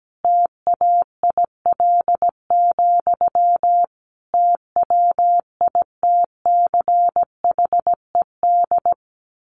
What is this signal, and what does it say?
Morse code, heard as a keyed tone, about 17 wpm, 700 Hz. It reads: TAIL, TWITCHED
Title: A Sound Of Thunder 17wpm